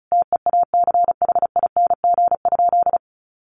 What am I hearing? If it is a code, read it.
Morse code, heard as a keyed tone, about 34 wpm, 700 Hz. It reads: TEACHING?
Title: Sentences from Top 1100 Words 35wpm